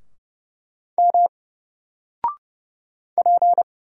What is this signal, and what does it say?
Morse code, heard as a keyed tone, about 29 wpm, 700 Hz. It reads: M P
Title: Single Letter-Number 30wpm